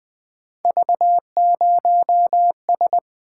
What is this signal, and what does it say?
Morse code, heard as a keyed tone, about 20 wpm, 700 Hz. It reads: V0S
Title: Three Letter-Number Combinations - 20wpm